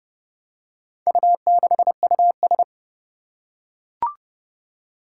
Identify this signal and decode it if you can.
Morse code, heard as a keyed tone, about 30 wpm, 700 Hz. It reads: U6US
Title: Four Challenging Letter-Number Combinations 30wpm